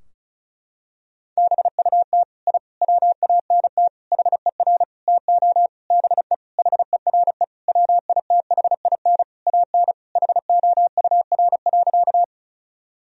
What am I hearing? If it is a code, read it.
Morse code, heard as a keyed tone, about 36 wpm, 700 Hz. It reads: BUT I WANT HER TO BE HERE WITHIN AN HOUR.
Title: Sentences from Top 1000 Words - Review Part 4 - 35wpm